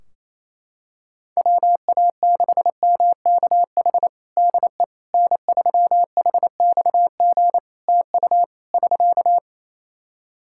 Morse code, heard as a keyed tone, about 27 wpm, 700 Hz. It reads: WA6MKH DE N3HXG TU <SK>
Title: Practice QSO - Exchange 4 - Ending - 28wpm